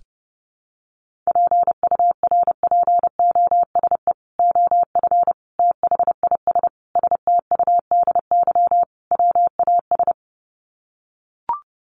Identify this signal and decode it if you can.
Morse code, heard as a keyed tone, about 30 wpm, 700 Hz. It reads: PURPOSE OF THIS STUDY WAS
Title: Sets of 5 Words 30wpm